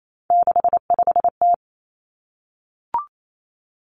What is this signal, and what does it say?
Morse code, heard as a keyed tone, about 28 wpm, 700 Hz. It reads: 65T
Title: Three Challenging Letter-Number Combinations 28wpm